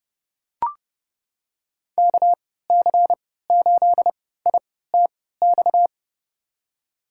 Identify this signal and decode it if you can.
Morse code, heard as a keyed tone, about 31 wpm, spaced as for 17 wpm, 700 Hz. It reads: KC8ITX
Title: USA Call Signs - 6 Characters - 3x Character Spacing - 30wpm